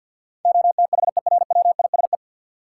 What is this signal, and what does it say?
Morse code, heard as a keyed tone, about 51 wpm, 700 Hz. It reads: OTHERWISE
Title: Sentences from Top 1200 Words - New Word Drill - 50wpm